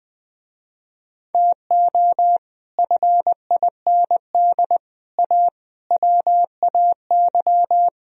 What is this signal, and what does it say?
Morse code, heard as a keyed tone, about 20 wpm, 700 Hz. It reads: TO FIND A WAY